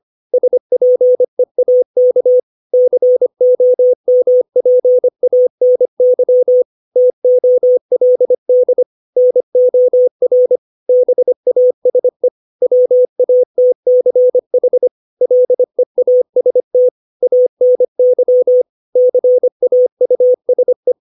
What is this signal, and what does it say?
Morse code, heard as a keyed tone, about 25 wpm, 500 Hz. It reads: SPEAK COMPANY TOLD NOR BASE WATCH LEAST ANY CAUSE
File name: 632 words / 632 Words 25WPM